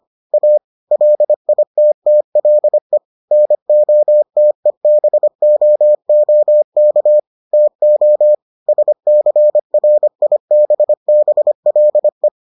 Morse code, heard as a keyed tone, about 25 wpm, 600 Hz. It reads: A LITTLE NOTEBOOK TO SCRIBBLE